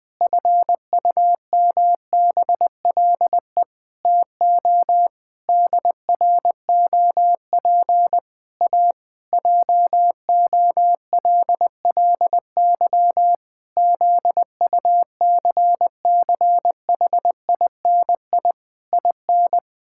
Morse code, heard as a keyed tone, about 20 wpm, 700 Hz. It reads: FUMBLE TO DROP A JOLLY ZUCCHINI IN
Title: Pangram Challenge v2 - 20wpm